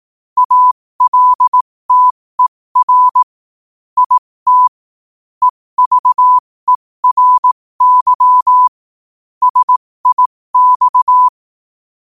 Morse code, heard as a keyed tone, about 18 wpm, spaced as for 15 wpm, 1000 Hz. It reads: ALTER IT EVERY SIX
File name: qotdfiles / 15wpm